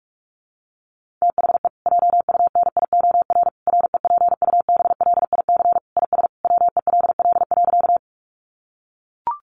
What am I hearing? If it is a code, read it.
Morse code, heard as a keyed tone, about 44 wpm, 700 Hz. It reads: THE JUNIOR REPUBLIC IS WELL.
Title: Sentences from Top 1500 Words 45wpm